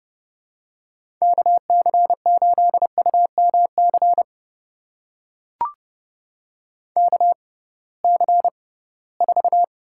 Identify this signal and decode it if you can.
Morse code, heard as a keyed tone, about 30 wpm, spaced as for 19 wpm, 700 Hz. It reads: KC8UMC KC4
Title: USA Call Signs - 6 Characters - 6x Character Spacing - 30wpm